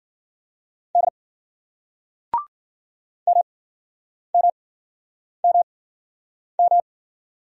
Morse code, heard as a keyed tone, about 56 wpm, spaced as for 9 wpm, 700 Hz. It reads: D MMMM